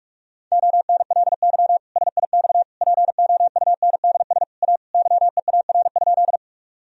Morse code, heard as a keyed tone, about 46 wpm, 700 Hz. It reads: ONLY SIX POUNDS A YEAR?
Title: Sentences from Top 1000 Words - Review Part 2 - 45wpm